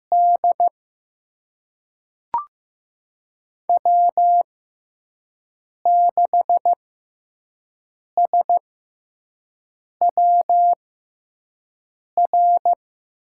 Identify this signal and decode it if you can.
Morse code, heard as a keyed tone, about 15 wpm, spaced as for 5 wpm, 700 Hz. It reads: D W6SWR
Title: USA Call Signs - 5 Characters - 6x Character Spacing - 15wpm